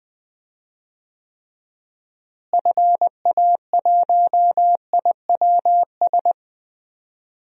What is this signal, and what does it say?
Morse code, heard as a keyed tone, about 20 wpm, 700 Hz. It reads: FA1IWS
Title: DXCC-100 Call Signs - 6 Characters - Repeated 3x - 20wpm